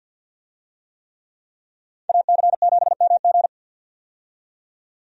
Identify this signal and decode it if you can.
Morse code, heard as a keyed tone, about 48 wpm, 700 Hz. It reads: AC7MG
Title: USA Call Signs - 5 Characters - 50wpm